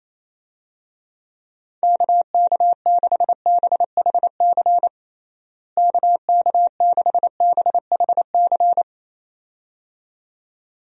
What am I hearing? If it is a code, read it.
Morse code, heard as a keyed tone, about 28 wpm, 700 Hz. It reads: KK6BHC KK6BHC
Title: USA Call Signs - 6 Characters - Repeated 2x - 28wpm